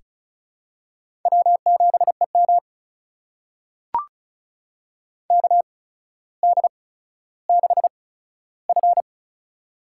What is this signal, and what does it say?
Morse code, heard as a keyed tone, about 36 wpm, spaced as for 15 wpm, 700 Hz. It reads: W7EM KD6F
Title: USA Call Signs - 4 Characters - 8x Character Spacing - 35wpm